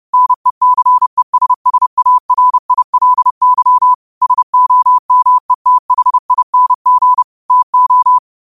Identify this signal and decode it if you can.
Morse code, heard as a keyed tone, about 30 wpm, 1000 Hz. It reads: NECESSARILY SOMETHING TO